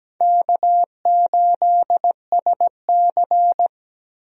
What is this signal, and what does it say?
Morse code, heard as a keyed tone, about 17 wpm, 700 Hz. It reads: K8SC
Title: USA Call Signs - 4 Characters - 17wpm